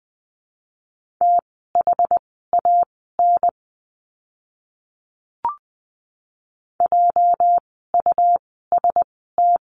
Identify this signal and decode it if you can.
Morse code, heard as a keyed tone, about 20 wpm, spaced as for 15 wpm, 700 Hz. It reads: THAN JUST
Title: 230 Lesson 37f - Letter x - Words 20wpm